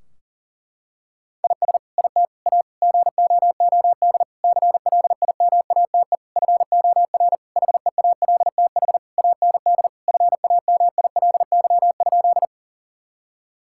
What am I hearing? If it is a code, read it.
Morse code, heard as a keyed tone, about 40 wpm, 700 Hz. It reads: IS IT A GOOD CLIMATE FOR HEALTH AND FAMILY?
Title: Sentences from Top 1600 Words 40wpm